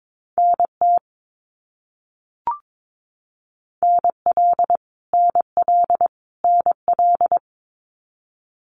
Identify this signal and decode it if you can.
Morse code, heard as a keyed tone, about 22 wpm, 700 Hz. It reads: NT NL NL NL
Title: Canadian Provinces and Territories - Repeated 3x - 22wpm